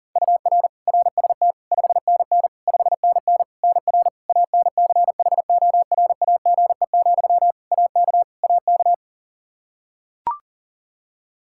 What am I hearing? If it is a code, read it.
Morse code, heard as a keyed tone, about 40 wpm, 700 Hz. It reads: UR RST 5NN 5NN NR ANCHORAGE, AK AK
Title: Practice QSO - Exchange 1 - Beginning - 40wpm